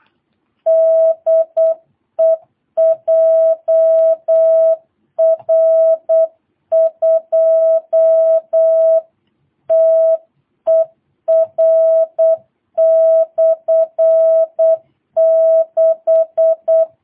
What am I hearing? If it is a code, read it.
Morse code, heard as a keyed tone, about 8 wpm, 650 Hz. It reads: DE JR2TER/6
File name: jh8cbh / 2022cw